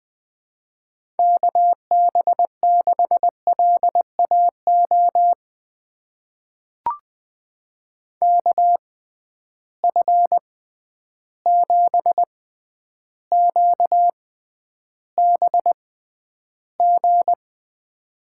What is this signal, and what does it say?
Morse code, heard as a keyed tone, about 20 wpm, spaced as for 10 wpm, 700 Hz. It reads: KB6LAO KF7QBG
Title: USA Call Signs - 6 Characters - 6x Character Spacing - 20wpm